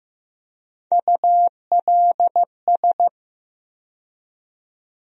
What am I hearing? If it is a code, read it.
Morse code, heard as a keyed tone, about 15 wpm, 700 Hz. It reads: ULS
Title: USA Call Sign Postfixes - 3 Letters 15wpm